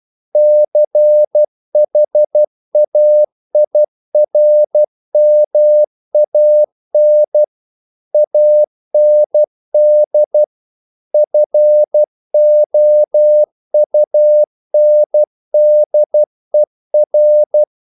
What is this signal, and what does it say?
Morse code, heard as a keyed tone, about 12 wpm, 600 Hz. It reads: CHAIRMAN AND FOUNDER
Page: English quotations in CW (telegraphy - morse)